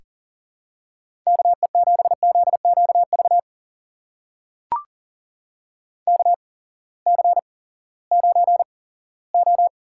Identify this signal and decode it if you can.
Morse code, heard as a keyed tone, about 37 wpm, spaced as for 19 wpm, 700 Hz. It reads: KE7ZQV KC9O
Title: USA Call Signs - 6 Characters - 8x Character Spacing - 40wpm